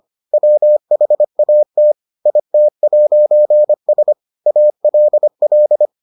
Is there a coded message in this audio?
Yes: Morse code from 0.3 to 5.9 s, heard as a keyed tone, about 25 wpm, 600 Hz. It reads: WHAT IT'S ALL